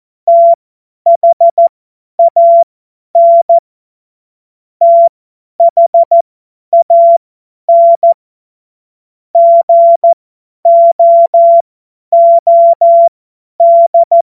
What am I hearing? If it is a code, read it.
Morse code, heard as a keyed tone, about 14 wpm, spaced as for 10 wpm, 680 Hz. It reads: THAN THAN GOOD
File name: Morsekurs / Witze / Witz 14. Dezember 2021, 50 F70